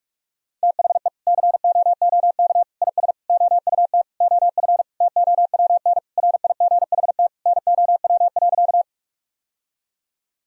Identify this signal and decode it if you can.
Morse code, heard as a keyed tone, about 46 wpm, 700 Hz. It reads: THE COOK IS OUT OF TOWN RIGHT NOW.
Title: Sentences from Top 1000 Words - Review Part 4 - 45wpm